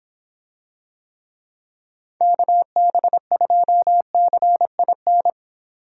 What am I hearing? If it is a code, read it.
Morse code, heard as a keyed tone, about 26 wpm, 700 Hz. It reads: KB2CIN